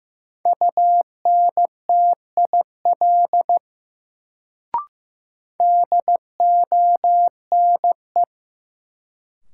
Morse code, heard as a keyed tone, about 15 wpm, 700 Hz. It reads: UNTIL DONE
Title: Sets of 5 Words Encore - New Word Drill - 15wpm